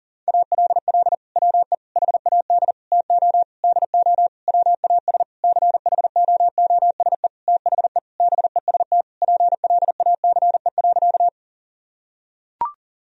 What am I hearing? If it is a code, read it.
Morse code, heard as a keyed tone, about 40 wpm, 700 Hz. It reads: ALL WE HAD TO DO WAS CHOOSE THE BEST PLACE.
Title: Sentences from Top 1000 Words - Review Part 4 - 40wpm